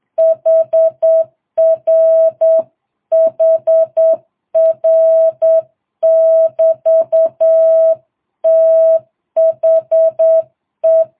Morse code, heard as a keyed tone, about 8 wpm, 650 Hz. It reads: HR HR <BT> THE